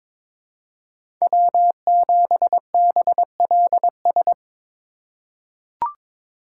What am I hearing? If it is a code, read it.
Morse code, heard as a keyed tone, about 22 wpm, 700 Hz. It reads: W7BLS
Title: USA Call Signs - 5 Characters - 3x Character Spacing - 22wpm